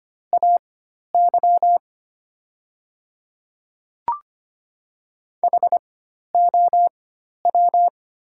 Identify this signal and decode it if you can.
Morse code, heard as a keyed tone, about 25 wpm, spaced as for 12 wpm, 690 Hz. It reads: AY HOW
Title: 4x Character Spacing - Top 100 Words - 25wpm